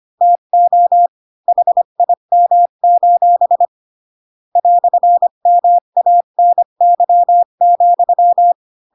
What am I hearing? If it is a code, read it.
Morse code, heard as a keyed tone, about 25 wpm, 700 Hz. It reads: TO HIM: "MANY,
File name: Pinocchio / Pinocchio ch5 20 25 ik1hge